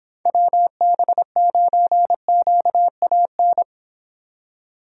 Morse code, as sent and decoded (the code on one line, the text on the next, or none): .-- -... ----. --.- .- -.
WB9QAN